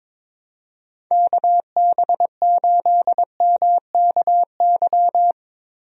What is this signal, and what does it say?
Morse code, heard as a keyed tone, about 22 wpm, 700 Hz. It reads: KB8MKY
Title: USA Call Signs - 6 Characters - Repeated 3x - 22wpm